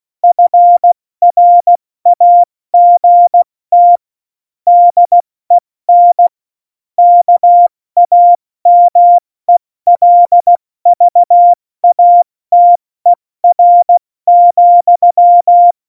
Morse code, heard as a keyed tone, about 16 wpm, 700 Hz. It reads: FRAGT DEN KAMELVATER,
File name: Morsekurs / Witze / Witz 06. November 2018, 70 F80